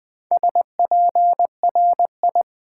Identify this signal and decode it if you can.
Morse code, heard as a keyed tone, about 20 wpm, 700 Hz. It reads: SPRI
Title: Build A Word - 2 Letters At A Time - 6-Letter Words - 20wpm